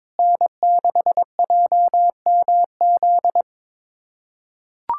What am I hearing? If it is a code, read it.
Morse code, heard as a keyed tone, about 22 wpm, 700 Hz. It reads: N6JMZ
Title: USA Callsign DE Callsign 22wpm